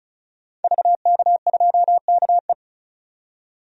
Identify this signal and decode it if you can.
Morse code, heard as a keyed tone, about 34 wpm, 700 Hz. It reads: VK2KE
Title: DXCC-100 Call Signs - 5 Characters - 35wpm